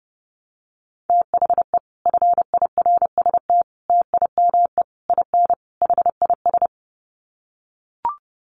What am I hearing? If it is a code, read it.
Morse code, heard as a keyed tone, about 30 wpm, 700 Hz. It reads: THE FIRST TIME IN HIS
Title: Sets of 5 Words Encore2 30wpm